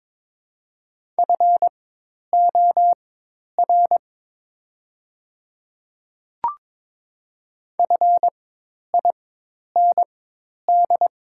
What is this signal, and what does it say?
Morse code, heard as a keyed tone, about 22 wpm, spaced as for 10 wpm, 700 Hz. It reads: FOR FIND
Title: 4x Character Spacing - Top 100 Words - 22wpm